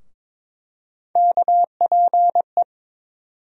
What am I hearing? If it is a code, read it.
Morse code, heard as a keyed tone, about 23 wpm, 700 Hz. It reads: KPE
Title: USA Call Sign Postfixes - 3 Letters 22wpm